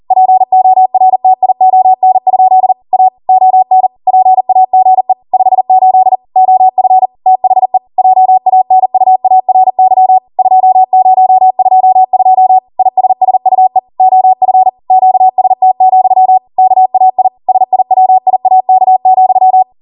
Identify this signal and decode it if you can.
Morse code, heard as a keyed tone, about 40 wpm, 750 Hz. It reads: PORTION? A ON PAGE 58 OF THE JANUARY 2022 ISSUE OF QST, KAI SIWIAK,
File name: Morse / 40WPM